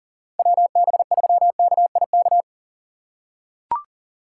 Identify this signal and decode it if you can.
Morse code, heard as a keyed tone, about 37 wpm, 700 Hz. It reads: WB3KIK
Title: USA Call Signs - 6 Characters - 2x Character Spacing - 40wpm